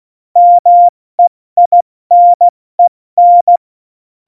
Morse code, heard as a keyed tone, about 16 wpm, 700 Hz. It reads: MEINEN
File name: Morsekurs / Witze / Witz 19. Januar 2021, 70 F80